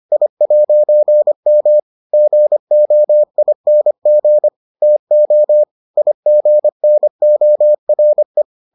Morse code, heard as a keyed tone, about 25 wpm, 600 Hz. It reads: I'M GOING TO IGNORE